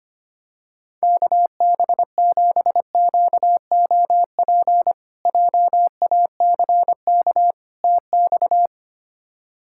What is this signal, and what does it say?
Morse code, heard as a keyed tone, about 25 wpm, 700 Hz. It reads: KB7QOP JACK TX
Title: K1USN SST QSO Practice - 25wpm